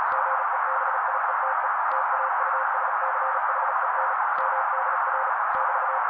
Morse code, heard as a keyed tone, about 35 wpm, 530 Hz. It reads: GEBIETE TAUSCHEN MDR <BT>
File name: daily / 35wpm noise